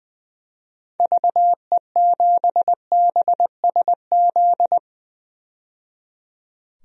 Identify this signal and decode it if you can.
Morse code, heard as a keyed tone, about 20 wpm, 700 Hz. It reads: VE7BSZ
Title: DXCC-100 Call Signs - 6 Characters - 20wpm